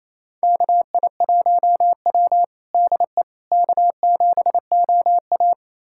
Morse code, heard as a keyed tone, about 28 wpm, 700 Hz. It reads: KI1W DE K7OA
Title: USA Callsign DE Callsign - 4 Characters - 28wpm